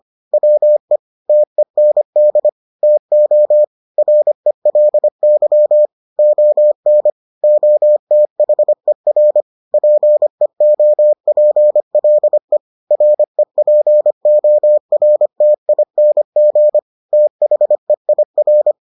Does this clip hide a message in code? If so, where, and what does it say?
Yes: Morse code from 0.3 to 18.7 s, heard as a keyed tone, about 25 wpm, 600 Hz. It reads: WE TEND TO RELY ON OTHER PEOPLE REPORTING THEIR